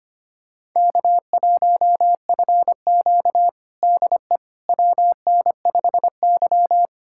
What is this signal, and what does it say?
Morse code, heard as a keyed tone, about 25 wpm, 700 Hz. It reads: K1FQ DE WN5Y
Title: USA Callsign DE Callsign - 4 Characters - 25wpm